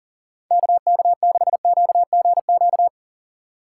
Morse code, heard as a keyed tone, about 40 wpm, 700 Hz. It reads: KK6QGQ
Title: USA Call Signs - 6 Characters - 4x Character Spacing - 40wpm